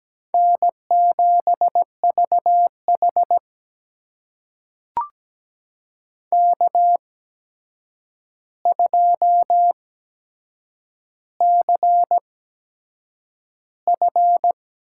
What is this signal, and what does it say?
Morse code, heard as a keyed tone, about 17 wpm, spaced as for 7 wpm, 700 Hz. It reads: N7VH K2CF
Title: USA Call Signs - 4 Characters - 8x Character Spacing - 17wpm